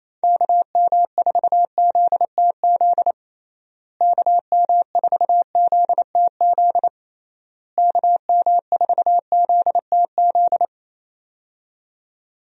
Morse code, heard as a keyed tone, about 28 wpm, 700 Hz. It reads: KM4ZTZ KM4ZTZ KM4ZTZ
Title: USA Call Signs - 6 Characters - Repeated 3x - 28wpm